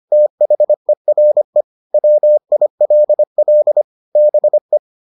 Morse code, heard as a keyed tone, about 25 wpm, 600 Hz. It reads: THERE WILL BE